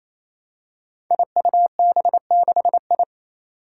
Morse code, heard as a keyed tone, about 28 wpm, 700 Hz. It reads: IUB6I
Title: Five Challenging Letter-Number Combinations 28wpm